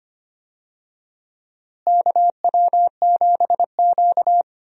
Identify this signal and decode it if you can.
Morse code, heard as a keyed tone, about 25 wpm, 700 Hz. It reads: KW7Q
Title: USA Call Signs - 4 Characters - Repeated 2x - 25wpm